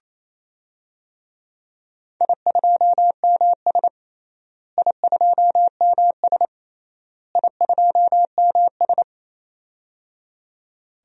Morse code, heard as a keyed tone, about 28 wpm, 700 Hz. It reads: I2MS I2MS I2MS
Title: DXCC-100 Call Signs - 4 Characters - Repeated 3x - 28wpm